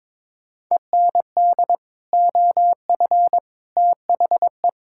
Morse code, heard as a keyed tone, about 21 wpm, 700 Hz. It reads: END OF THE
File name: practice / 22wpm / Sets of 8 Words - 22wpm